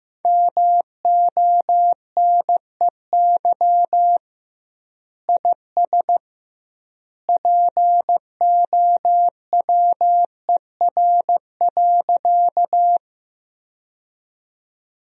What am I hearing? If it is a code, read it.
Morse code, heard as a keyed tone, about 15 wpm, 700 Hz. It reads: MONEY IS POWER.
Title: 2x Word-Spacing - Sentences from Top 500 Words - 15wpm